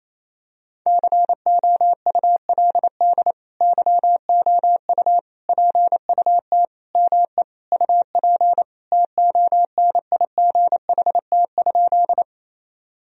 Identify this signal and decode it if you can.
Morse code, heard as a keyed tone, about 28 wpm, 700 Hz. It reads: COULD YOU PUT ME UP TONIGHT?
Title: Sentences from Top 1000 Words - Review Part 3 - 28wpm